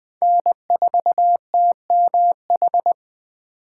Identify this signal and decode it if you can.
Morse code, heard as a keyed tone, about 20 wpm, 700 Hz. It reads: N4TMH
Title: Speed Racing - USA Call Signs 17wpm